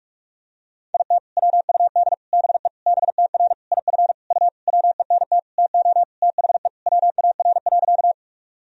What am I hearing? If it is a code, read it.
Morse code, heard as a keyed tone, about 46 wpm, 700 Hz. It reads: IT WUD BE BTR IF U WENT TO THE WAR.
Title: Sentences from Top 400 Words and Abbreviations 45wpm